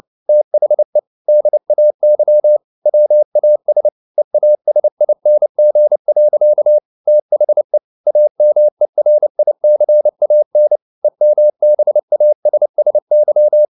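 Morse code, heard as a keyed tone, about 30 wpm, 600 Hz. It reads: THE DAY WAS EASING. THE AMERICAN EMBASSY
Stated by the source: Morse code at 29 wpm, sending Ken Follett's novel Never in English, from CW med Gnister.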